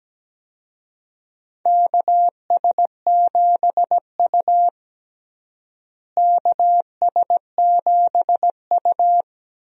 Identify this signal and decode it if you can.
Morse code, heard as a keyed tone, about 17 wpm, 700 Hz. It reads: KS7U KS7U
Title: USA Call Signs - 4 Characters - Repeated 3x - 17wpm